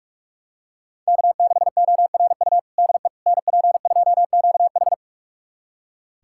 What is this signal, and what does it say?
Morse code, heard as a keyed tone, about 41 wpm, 700 Hz. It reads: K6ORU DE NP2QH
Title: USA Callsign DE Callsign - 5 Characters - 45wpm